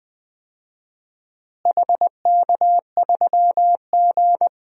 Morse code, heard as a keyed tone, about 20 wpm, 700 Hz. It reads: HK3G
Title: DXCC-100 Call Signs - 4 Characters - Repeated 3x - 20wpm